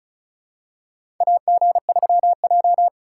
Morse code, heard as a keyed tone, about 36 wpm, 700 Hz. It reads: AG3J
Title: USA Call Signs - 4 Characters - 3x Character Spacing - 35wpm